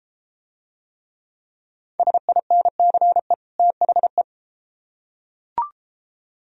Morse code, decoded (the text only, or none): SINCE THE